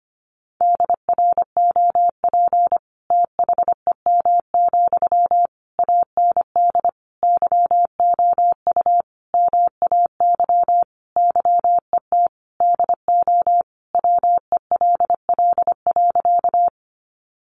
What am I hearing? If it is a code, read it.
Morse code, heard as a keyed tone, about 25 wpm, 700 Hz. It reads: DROP THEM, AND YOU MAY YET DO WELL.
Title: Sentences from Top 1400 Words 25wpm